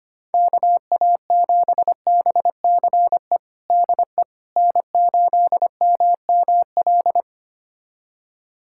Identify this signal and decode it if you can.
Morse code, heard as a keyed tone, about 25 wpm, 700 Hz. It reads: KA7BCE DE N8MML
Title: USA Callsign DE Callsign 25wpm